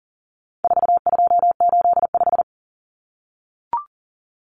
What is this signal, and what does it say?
Morse code, heard as a keyed tone, about 40 wpm, 700 Hz. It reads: 4285
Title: Common Cut Numbers - 4-Digit 40wpm